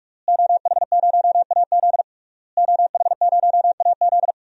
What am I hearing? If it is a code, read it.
Morse code, heard as a keyed tone, about 46 wpm, 700 Hz. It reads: OH0AZ OH0AZ
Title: DXCC-100 Call Signs - 5 Characters - Repeated 3x - 45wpm